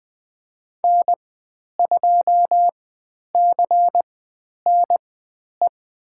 Morse code, heard as a keyed tone, about 20 wpm, spaced as for 10 wpm, 700 Hz. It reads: N2CNE
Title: 102 Lesson 17g - Letter u - Call Signs 20wpm